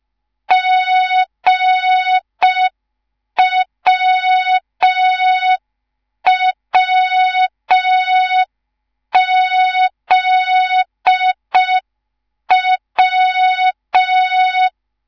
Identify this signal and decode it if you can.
Morse code, heard as a keyed tone, about 5 wpm, 3700 Hz. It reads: GWWZW
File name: Morze-HA2MN / hang / 10GWZ HA2MN